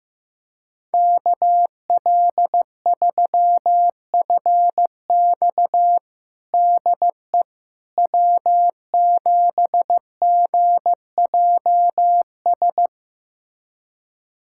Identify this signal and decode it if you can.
Morse code, heard as a keyed tone, about 15 wpm, 700 Hz. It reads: KL3FX DE W7GJS
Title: USA Callsign DE Callsign - 5 Characters - 15wpm